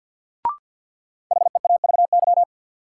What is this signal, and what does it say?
Morse code, heard as a keyed tone, about 53 wpm, 700 Hz. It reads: HEAVY
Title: Sentences from Top 1000 Words - Review Part 6 - New Word Drill - 50wpm